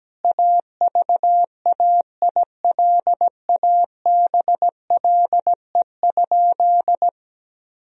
Morse code, heard as a keyed tone, about 17 wpm, 700 Hz. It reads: AVAILABLE?
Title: Sentences from Top 1000 Words - Review Part 3 - 17wpm